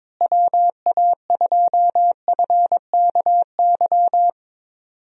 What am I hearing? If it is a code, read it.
Morse code, heard as a keyed tone, about 22 wpm, 700 Hz. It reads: WA2FKY
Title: USA Call Signs - 6 Characters - Repeated 3x - 22wpm